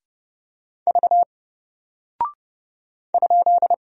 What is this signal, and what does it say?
Morse code, heard as a keyed tone, about 31 wpm, 700 Hz. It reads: V ?
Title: Single Character (Letter:Number:Punctuation:Prosign) Alt 30wpm